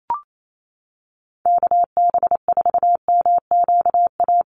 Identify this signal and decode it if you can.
Morse code, heard as a keyed tone, about 28 wpm, 700 Hz. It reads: KB4MQA
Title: USA Call Signs Repeated 3x 28wpm